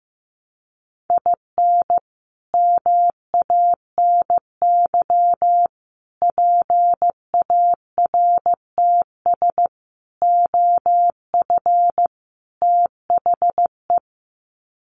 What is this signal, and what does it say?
Morse code, heard as a keyed tone, about 15 wpm, 700 Hz. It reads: IN MANY PARTS OF THE
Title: Sets of 5 Words Encore 15wpm